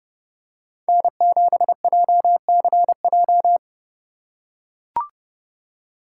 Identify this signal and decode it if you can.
Morse code, heard as a keyed tone, about 30 wpm, 700 Hz. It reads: N7JCJ